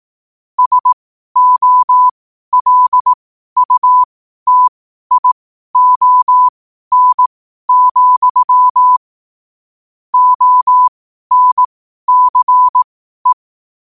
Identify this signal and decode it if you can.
Morse code, heard as a keyed tone, about 18 wpm, spaced as for 12 wpm, 1000 Hz. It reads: SOLUTION, ONCE
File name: qotdfiles / 12wpm